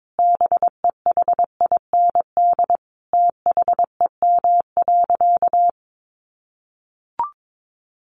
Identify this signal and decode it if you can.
Morse code, heard as a keyed tone, about 22 wpm, 700 Hz. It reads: BEHIND THEM.
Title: Sentences from Top 1500 Words 22wpm